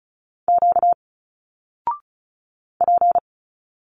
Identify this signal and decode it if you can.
Morse code, heard as a keyed tone, about 34 wpm, 700 Hz. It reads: Q P
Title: Single Letters 35wpm